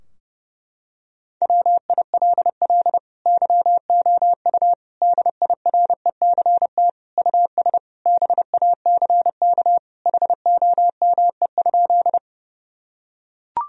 Morse code, heard as a keyed tone, about 30 wpm, 700 Hz. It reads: WILL YOU DIRECT US BACK HOME?
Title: Sentences from Top 1000 Words - Review Part 2 - 30wpm